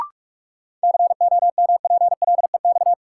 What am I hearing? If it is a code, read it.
Morse code, heard as a keyed tone, about 44 wpm, 700 Hz. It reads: COMPLEX
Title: Sentences from Top 900 Words - New Word Drill - 45wpm